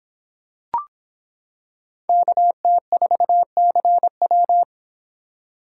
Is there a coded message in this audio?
Yes: Morse code from 2.1 to 4.6 s, heard as a keyed tone, about 25 wpm, 700 Hz. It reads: KT4CW